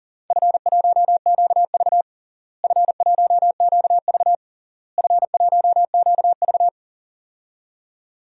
Morse code, heard as a keyed tone, about 40 wpm, 700 Hz. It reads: F1QV F1QV F1QV
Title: DXCC-100 Call Signs - 4 Characters - Repeated 3x - 40wpm